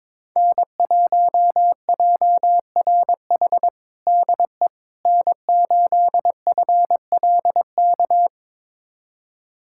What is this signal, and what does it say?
Morse code, heard as a keyed tone, about 22 wpm, 700 Hz. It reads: N1JRH DE N8FLK
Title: USA Callsign DE Callsign - 5 Characters - 22wpm